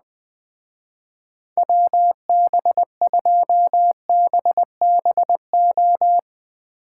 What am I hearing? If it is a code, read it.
Morse code, heard as a keyed tone, about 20 wpm, 700 Hz. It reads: WB2BBO
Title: Speed Racing - USA 2x3 Call Signs 17wpm